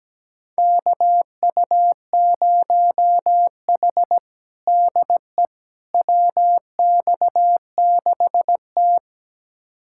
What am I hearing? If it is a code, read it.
Morse code, heard as a keyed tone, about 17 wpm, 700 Hz. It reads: KU0H DE WX6T
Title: USA Callsign DE Callsign - 4 Characters - 17wpm